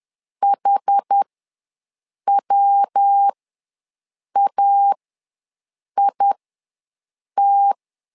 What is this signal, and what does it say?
Morse code, heard as a keyed tone, about 10 wpm, spaced as for 6 wpm, 800 Hz. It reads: HWAIT
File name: LEZIONI / 3 MWRO / 3 MW ws 10wpm 2213